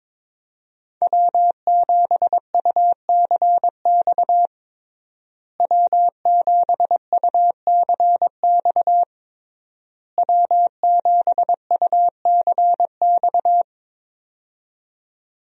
Morse code, heard as a keyed tone, about 22 wpm, 700 Hz. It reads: W7UCX W7UCX W7UCX
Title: USA Call Signs - 5 Characters - Repeated 3x - 22wpm